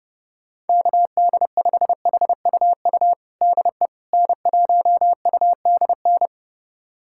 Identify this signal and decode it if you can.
Morse code, heard as a keyed tone, about 30 wpm, 700 Hz. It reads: KD5HUU DE N1UDN
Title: USA Callsign DE Callsign 30wpm